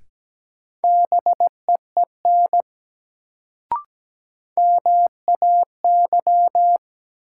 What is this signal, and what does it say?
Morse code, heard as a keyed tone, about 17 wpm, 700 Hz. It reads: BEEN MAY
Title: Sets of 2 Words - New Word Drill - 17wpm